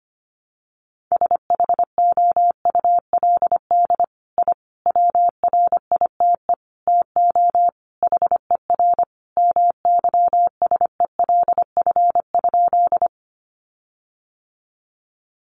Morse code, heard as a keyed tone, about 25 wpm, 700 Hz. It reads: SHOULD I WRITE TO HER MYSELF?
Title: Sentences from Top 1300 Words 25wpm